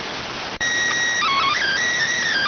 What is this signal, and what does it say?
SSTV (slow-scan television), PD 290.